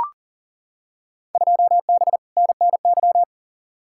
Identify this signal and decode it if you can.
Morse code, heard as a keyed tone, about 40 wpm, 700 Hz. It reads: 2B NNY
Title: Field Day Exchanges Encore - 40wpm